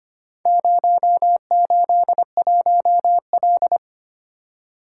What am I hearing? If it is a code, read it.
Morse code, heard as a keyed tone, about 25 wpm, 690 Hz. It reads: 081L
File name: practice / 25wpm / Four Letter-Number Combinations - 25wpm